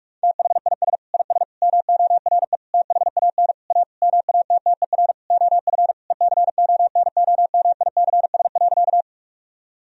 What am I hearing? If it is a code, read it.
Morse code, heard as a keyed tone, about 42 wpm, 700 Hz. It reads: THIS IS MORE THAN A MATTER OF ECONOMICS.
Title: Sentences from Top 1000 Words - Review Part 3 - 45wpm